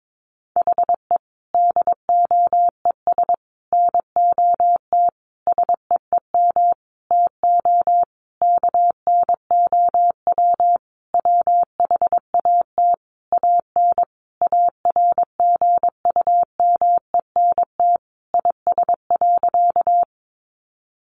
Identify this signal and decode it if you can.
Morse code, heard as a keyed tone, about 22 wpm, 700 Hz. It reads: HE DOES NOT SEEM TO KNOW WHAT AN ARGUMENT IS.
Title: Sentences from Top 1400 Words 22wpm